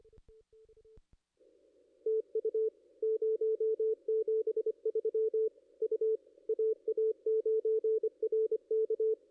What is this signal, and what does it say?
Morse code, heard as a keyed tone, about 25 wpm, 440 Hz, with radio noise. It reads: TU 073 U AA9RK
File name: ss-cw-k5tr-op-n5rz